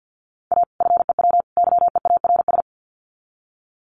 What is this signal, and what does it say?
Morse code, heard as a keyed tone, about 49 wpm, 700 Hz. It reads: A FEW YEARS
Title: Sets of 3 Words 50wpm